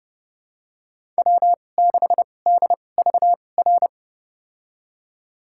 Morse code, heard as a keyed tone, about 32 wpm, spaced as for 22 wpm, 700 Hz. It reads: W6DVR
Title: USA Call Signs - 2x Character Spacing - 30wpm